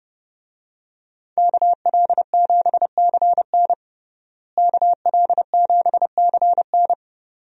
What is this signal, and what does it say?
Morse code, heard as a keyed tone, about 30 wpm, 700 Hz. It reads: KL7CN KL7CN
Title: USA Call Signs - 5 Characters - Repeated 2x - 30wpm